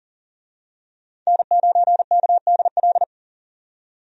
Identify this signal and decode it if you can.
Morse code, heard as a keyed tone, about 40 wpm, 700 Hz. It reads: N9KDL
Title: USA Call Signs - 5 Characters - 40wpm